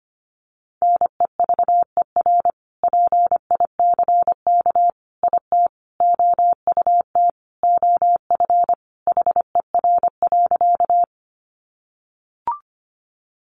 Morse code, heard as a keyed tone, about 25 wpm, 700 Hz. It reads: NEVER PICK IT OUT OF HER.
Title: Sentences from Top 1000 Words 25wpm